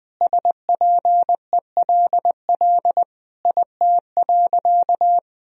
Morse code, heard as a keyed tone, about 20 wpm, 700 Hz. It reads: SPELL IT.
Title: Sentences from Top 1000 Words - Review Part 1 - 20wpm